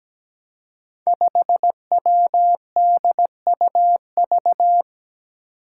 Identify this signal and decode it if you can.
Morse code, heard as a keyed tone, about 17 wpm, 700 Hz. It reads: 5WDUV
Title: Five Challenging Letter-Number Combinations 17wpm